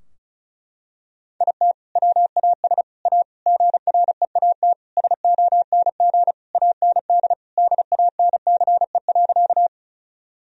Morse code, heard as a keyed tone, about 34 wpm, 700 Hz. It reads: IT WAS A GREAT SONG AND DANCE.
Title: Sentences from Top 1000 Words - Review Part 4 - 35wpm